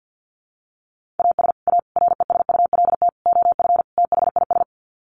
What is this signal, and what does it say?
Morse code, heard as a keyed tone, about 50 wpm, 700 Hz. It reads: AS A RESULT OF THIS
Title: Sets of 5 Words 50wpm